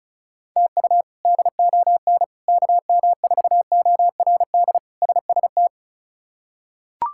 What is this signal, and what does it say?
Morse code, heard as a keyed tone, about 36 wpm, 700 Hz. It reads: TU DON KM4ORD SST
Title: K1USN SST QSO Practice - 35wpm